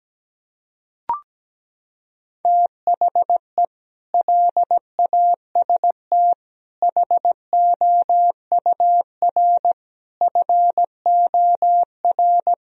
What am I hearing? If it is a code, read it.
Morse code, heard as a keyed tone, about 17 wpm, 700 Hz. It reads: THE LAST HOUR FOR
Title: Sentences from Top 1000 Words - Review Part 4 - 17wpm